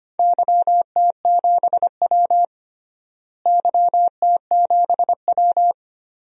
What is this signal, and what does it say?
Morse code, heard as a keyed tone, about 25 wpm, 700 Hz. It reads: YT7W YT7W
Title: DXCC-100 Call Signs - 4 Characters - Repeated 2x - 25wpm